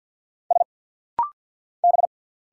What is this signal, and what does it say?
Morse code, heard as a keyed tone, about 50 wpm, 700 Hz. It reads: S B
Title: Single Letters - Rapid-Fire 50wpm